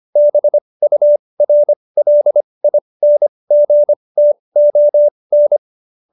Morse code, heard as a keyed tone, about 25 wpm, spaced as for 20 wpm, 590 Hz. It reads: BURLINGTON